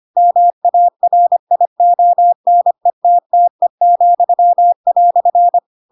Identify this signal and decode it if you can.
Morse code, heard as a keyed tone, about 25 wpm, 700 Hz. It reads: MARIONETTE,"
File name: Pinocchio / Pinocchio ch3 25 25 ik1hge